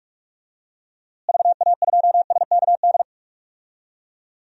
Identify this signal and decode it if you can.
Morse code, heard as a keyed tone, about 44 wpm, 700 Hz. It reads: VA2SKD
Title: DXCC-100 Call Signs - 6 Characters - Repeated 2x - 45wpm